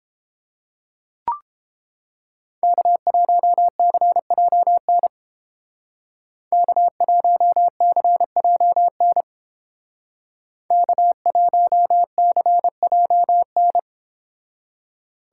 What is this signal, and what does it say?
Morse code, heard as a keyed tone, about 29 wpm, 700 Hz. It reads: K1CJN K1CJN K1CJN